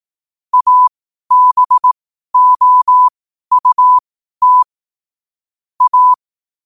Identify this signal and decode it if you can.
Morse code, heard as a keyed tone, about 18 wpm, spaced as for 12 wpm, 1000 Hz. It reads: ABOUT A
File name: qotdfiles / 12wpm